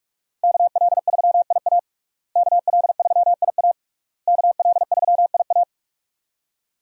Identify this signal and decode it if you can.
Morse code, heard as a keyed tone, about 46 wpm, 700 Hz. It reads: KL3IA KL3IA KL3IA
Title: USA Call Signs - 5 Characters - Repeated 3x - 45wpm